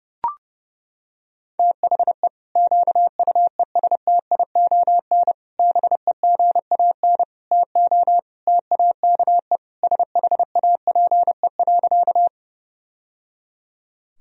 Morse code, heard as a keyed tone, about 30 wpm, 700 Hz. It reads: THE QUESTION BEGAN TO TAKE SHAPE.
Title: Sentences from Top 1000 Words - Review Part 3 - 30wpm